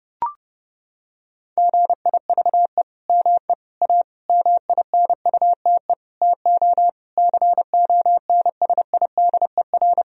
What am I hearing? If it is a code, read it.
Morse code, heard as a keyed tone, about 30 wpm, 700 Hz. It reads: GIVE ME A MINUTE TO CONSIDER
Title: Sentences from Top 1000 Words - Review Part 4 - 30wpm